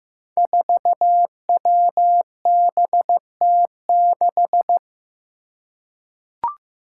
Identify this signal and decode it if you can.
Morse code, heard as a keyed tone, about 15 wpm, 700 Hz. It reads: 4WBT6
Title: Five Challenging Letter-Number Combinations 15wpm